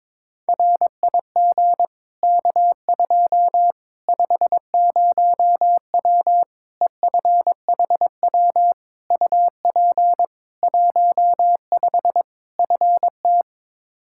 Morse code, heard as a keyed tone, about 22 wpm, 700 Hz. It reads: RIG K2 50W EFHW UP 15 FT
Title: Practice QSO - Exchange 2 - Info - 22wpm